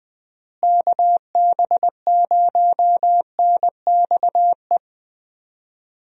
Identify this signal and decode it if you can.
Morse code, heard as a keyed tone, about 20 wpm, 700 Hz. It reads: KB0NXE
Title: USA Call Signs - 6 Characters - Repeated 3x - 20wpm